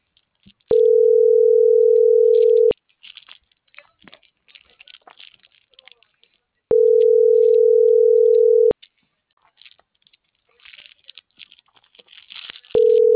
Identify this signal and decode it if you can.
Ringback tone (North America), 440 + 480 Hz.